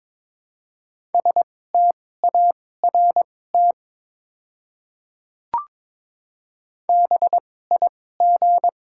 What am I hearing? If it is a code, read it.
Morse code, heard as a keyed tone, about 22 wpm, spaced as for 16 wpm, 700 Hz. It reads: START BIG
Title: 2x Character Spacing - Top 300 Words - 22wpm